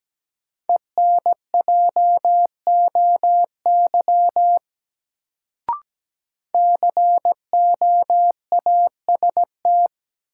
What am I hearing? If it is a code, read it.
Morse code, heard as a keyed tone, about 17 wpm, 700 Hz. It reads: ENJOY COAST
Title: Sentences from Top 1300 Words - New Word Drill Alt - 17wpm